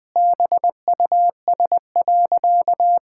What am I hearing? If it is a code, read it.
Morse code, heard as a keyed tone, about 21 wpm, 700 Hz. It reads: BUS.
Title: Sentences from Top 1000 Words - Review Part 6 - 20wpm